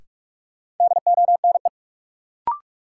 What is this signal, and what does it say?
Morse code, heard as a keyed tone, about 45 wpm, 700 Hz. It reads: DONE
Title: Sentences from Top 500 Words - New Word Drill - 45wpm